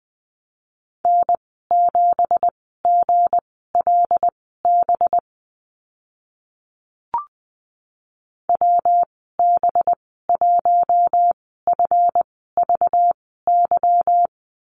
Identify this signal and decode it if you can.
Morse code, heard as a keyed tone, about 20 wpm, spaced as for 14 wpm, 700 Hz. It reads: N7GLB WB1FVY
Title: 193 Lesson 31g - Letter v - Call Signs 20wpm